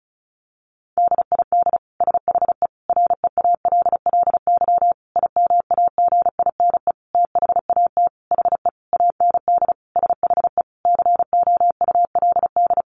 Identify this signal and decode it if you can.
Morse code, heard as a keyed tone, about 34 wpm, 700 Hz. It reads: DID SHE REALLY IMAGINE THAT HE AND SHE COULD
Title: Sentences from Top 1400 Words 35wpm